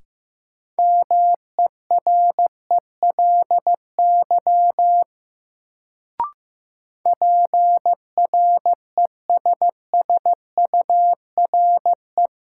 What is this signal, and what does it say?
Morse code, heard as a keyed tone, about 15 wpm, 700 Hz. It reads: MERELY PRESSURE
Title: Sentences from Top 1400 Words - New Word Drill Alt - 15wpm